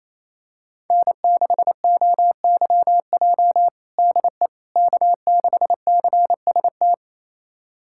Morse code, heard as a keyed tone, about 28 wpm, 700 Hz. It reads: N6OYJ DE K6CST
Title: USA Callsign DE Callsign - 5 Characters - 28wpm